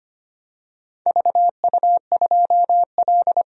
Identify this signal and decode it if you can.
Morse code, heard as a keyed tone, about 25 wpm, 700 Hz. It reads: VU2L
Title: DXCC-100 Call Signs - 4 Characters - 25wpm